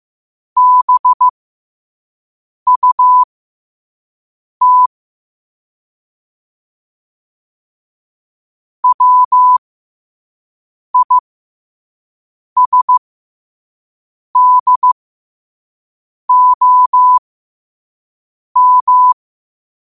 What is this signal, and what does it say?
Morse code, heard as a keyed tone, about 15 wpm, spaced as for 5 wpm, 1000 Hz. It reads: BUT WISDOM
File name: qotdfiles / 5wpm